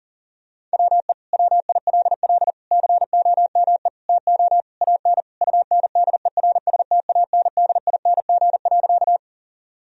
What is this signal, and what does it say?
Morse code, heard as a keyed tone, about 37 wpm, 700 Hz. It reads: WE WILL COME TO AN UNDERSTANDING.
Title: Sentences from Top 1000 Words - Review Part 3 - 40wpm